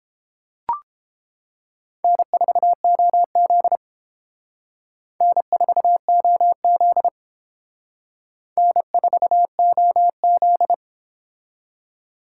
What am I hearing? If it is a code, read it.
Morse code, heard as a keyed tone, about 30 wpm, 700 Hz. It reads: N4OZ N4OZ N4OZ